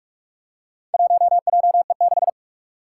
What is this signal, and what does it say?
Morse code, heard as a keyed tone, about 45 wpm, 700 Hz. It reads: 1JE6
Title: Four Challenging Letter-Number Combinations 45wpm